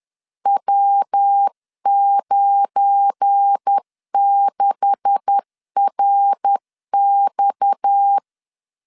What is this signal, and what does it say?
Morse code, heard as a keyed tone, about 10 wpm, 800 Hz. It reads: W96RX
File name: LEZIONI / 9 286 10wpm 2735